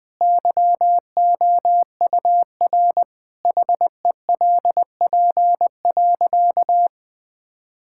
Morse code, heard as a keyed tone, about 20 wpm, 700 Hz. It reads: YOUR HELP.
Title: Sentences with 4-Word Phrases - 20wpm